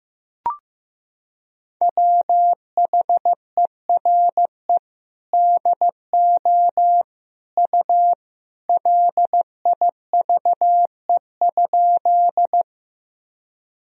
Morse code, heard as a keyed tone, about 15 wpm, 700 Hz. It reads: WHERE DO U LIVE?
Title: Sentences from Top 200 Words and Abbreviations 15wpm